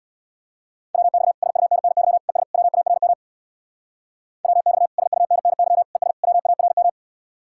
Morse code, heard as a keyed tone, about 60 wpm, 700 Hz. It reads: WX SUNNY ES WARM WX SUNNY ES WARM
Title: Speed-Racing Common QSO Phrases - 50wpm